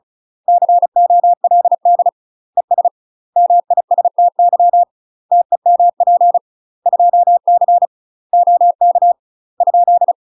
Morse code, heard as a keyed tone, about 34 wpm, 700 Hz. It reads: COLD ES MISTY TEMP 2C OK ?
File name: UK MASTER QSOs TRAINING / 35WPM / UK MASTER QSOs 0034